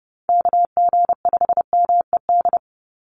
Morse code, heard as a keyed tone, about 30 wpm, 700 Hz. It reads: KG5MED
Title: USA Call Signs 30wpm